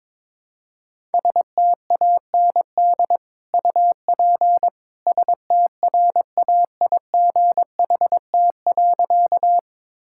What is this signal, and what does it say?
Morse code, heard as a keyed tone, about 22 wpm, 700 Hz. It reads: STAND UP STRAIGHT.
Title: Sentences from Top 1000 Words - Review Part 6 - 22wpm